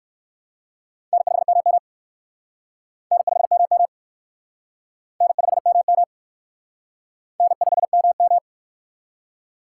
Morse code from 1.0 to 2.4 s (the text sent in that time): N5MM